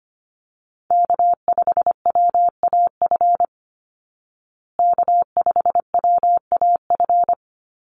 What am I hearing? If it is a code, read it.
Morse code, heard as a keyed tone, about 25 wpm, 700 Hz. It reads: K5WAF K5WAF
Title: Call Signs Repeated 2x - 25wpm